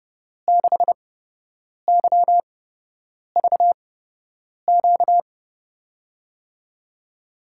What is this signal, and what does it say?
Morse code, heard as a keyed tone, about 32 wpm, spaced as for 8 wpm, 700 Hz. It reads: 6YVQ
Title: USA Call Signs - 6 Characters - 8x Character Spacing - 30wpm